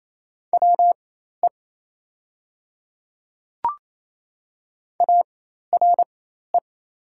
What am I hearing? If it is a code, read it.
Morse code, heard as a keyed tone, about 28 wpm, spaced as for 13 wpm, 700 Hz. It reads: WE ARE
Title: 4x Character Spacing - Top 100 Words - 28wpm